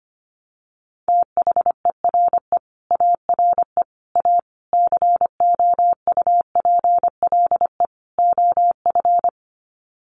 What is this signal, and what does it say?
Morse code, heard as a keyed tone, about 25 wpm, 690 Hz. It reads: THERE ARE A COUPLE OF
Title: Sets of 5 Words Encore 25wpm